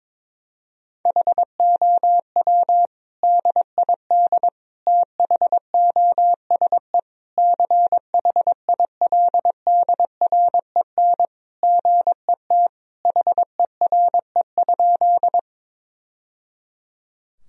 Morse code, heard as a keyed tone, about 22 wpm, 700 Hz. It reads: HOW DID THOSE CHILDREN GET HERE?
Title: Sentences from Top 1000 Words - Review Part 1 - 22wpm